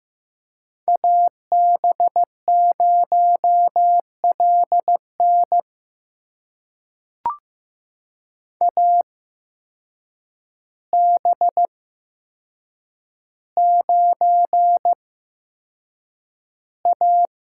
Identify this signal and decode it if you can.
Morse code, heard as a keyed tone, about 15 wpm, spaced as for 7 wpm, 700 Hz. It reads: AB0LN AB9A
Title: USA Call Signs - 5 Characters - 8x Character Spacing - 15wpm